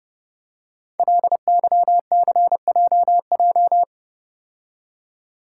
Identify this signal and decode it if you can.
Morse code, heard as a keyed tone, about 31 wpm, 700 Hz. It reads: LYCJJ
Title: Code Groups - 5 Letter:Numbers - 30wpm